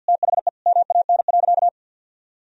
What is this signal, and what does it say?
Morse code, heard as a keyed tone, about 51 wpm, 700 Hz. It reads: THE MAN.
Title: Sentences from Top 200 Words and Abbreviations 50wpm